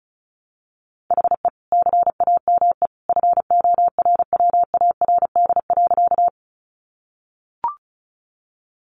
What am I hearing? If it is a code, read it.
Morse code, heard as a keyed tone, about 34 wpm, 700 Hz. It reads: HE CAME FORWARD.
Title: Sentences from Top 1300 Words 35wpm